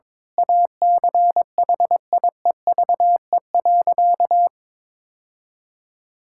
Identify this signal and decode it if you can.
Morse code, heard as a keyed tone, about 21 wpm, 700 Hz. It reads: ACHIEVE.
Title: Sentences from Top 1000 Words - Review Part 6 - 22wpm